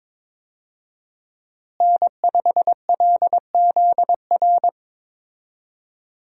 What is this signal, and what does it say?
Morse code, heard as a keyed tone, about 22 wpm, 700 Hz. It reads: N5LZR
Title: USA Call Signs - 5 Characters - Repeated 3x - 22wpm